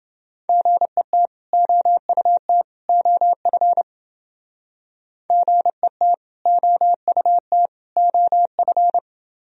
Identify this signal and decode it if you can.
Morse code, heard as a keyed tone, about 30 wpm, 700 Hz. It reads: GET OUT OF GET OUT OF